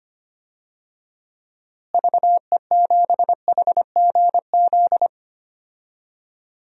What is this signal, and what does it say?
Morse code, heard as a keyed tone, about 25 wpm, 700 Hz. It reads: VE7HGZ
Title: DXCC-100 Call Signs - 6 Characters - Repeated 2x - 25wpm